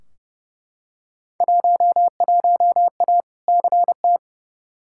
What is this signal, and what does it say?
Morse code, heard as a keyed tone, about 29 wpm, 700 Hz. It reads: 11A CT
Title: Field Day Exchanges Encore - 30wpm